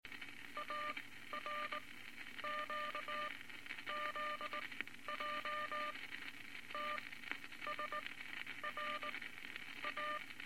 Morse code, heard as a keyed tone, about 20 wpm, 1200 Hz, with noise. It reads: A R Q Z J T S R A
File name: Morse Code Records / 19 B LCM 167 Twelve Words Per Minute